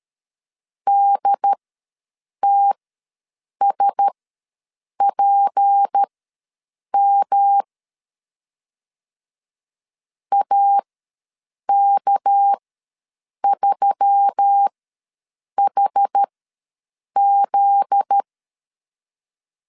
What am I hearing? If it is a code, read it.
Morse code, heard as a keyed tone, about 13 wpm, spaced as for 7 wpm, 800 Hz. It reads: DTSPM AK3HZ
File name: LEZIONI / 9 2 ws 12wpm 3406